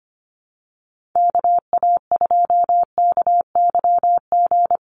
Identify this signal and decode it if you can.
Morse code, heard as a keyed tone, about 25 wpm, 690 Hz. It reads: KA2KYG
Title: USA Call Signs 25wpm